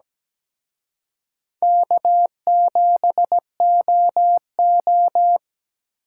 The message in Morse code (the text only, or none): K7OO